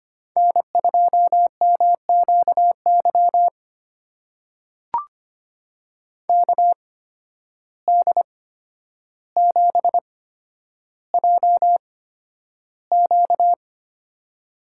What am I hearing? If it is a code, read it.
Morse code, heard as a keyed tone, about 25 wpm, spaced as for 11 wpm, 700 Hz. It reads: N2MQY KD7JQ
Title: USA Call Signs - 5 Characters - 8x Character Spacing - 25wpm